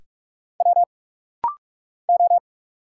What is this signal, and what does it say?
Morse code, heard as a keyed tone, about 44 wpm, 700 Hz. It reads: W O
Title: Single Letters - Rapid-Fire 45wpm